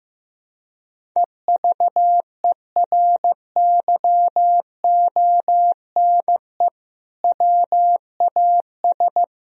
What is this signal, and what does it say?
Morse code, heard as a keyed tone, about 16 wpm, 700 Hz. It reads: EVERYONE WAS
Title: Sentences from Top 1000 Words - Review Part 1 - 15wpm